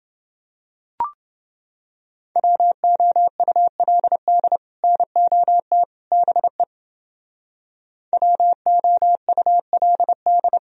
Morse code, heard as a keyed tone, about 29 wpm, 700 Hz. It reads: WOULD NOT BE WOULD